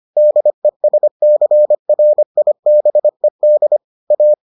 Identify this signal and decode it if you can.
Morse code, heard as a keyed tone, about 25 wpm, 600 Hz. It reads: DESCRIBED A